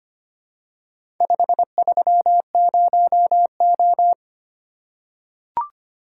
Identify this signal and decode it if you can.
Morse code, heard as a keyed tone, about 25 wpm, 700 Hz. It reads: 530O
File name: practice / 25wpm / Four Letter-Number Combinations - 25wpm